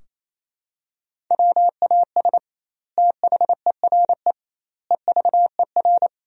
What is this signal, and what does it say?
Morse code, heard as a keyed tone, about 28 wpm, 700 Hz. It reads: WAS THERE EVER
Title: 2x Word-Spacing - Sentences from Top 400 Words - 28wpm